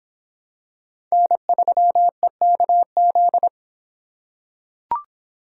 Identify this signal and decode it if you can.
Morse code, heard as a keyed tone, about 25 wpm, 700 Hz. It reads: N3EKZ